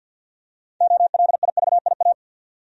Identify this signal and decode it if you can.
Morse code, heard as a keyed tone, about 53 wpm, 700 Hz. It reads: OLIVIA
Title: USA Common Names - 50wpm